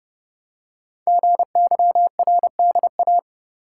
Morse code, heard as a keyed tone, about 30 wpm, 700 Hz. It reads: GYRDA
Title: Code Groups - 5 Letters - 30wpm